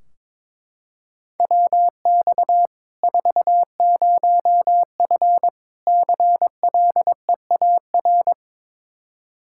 Morse code, heard as a keyed tone, about 22 wpm, 700 Hz. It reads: WX 40F CLEAR
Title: Practice QSO - Exchange 2 - Info - 22wpm